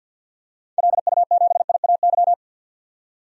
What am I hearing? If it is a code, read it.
Morse code, heard as a keyed tone, about 45 wpm, 700 Hz. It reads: LU7IAY
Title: DXCC-100 Call Signs - 6 Characters - 50wpm